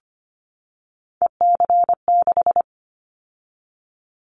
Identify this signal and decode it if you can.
Morse code, heard as a keyed tone, about 25 wpm, 700 Hz. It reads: EC6
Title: Callsign Prefixes for Top 100 DXCC - 25wpm